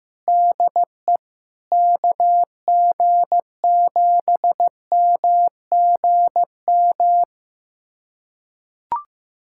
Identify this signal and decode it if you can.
Morse code, heard as a keyed tone, about 15 wpm, 700 Hz. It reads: DE KG7MGM
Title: USA Callsign DE Callsign 15wpm